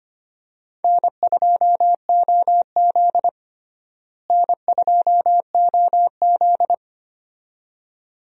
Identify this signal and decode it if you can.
Morse code, heard as a keyed tone, about 25 wpm, 700 Hz. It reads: N2OZ N2OZ
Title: USA Call Signs - 4 Characters - Repeated 2x - 25wpm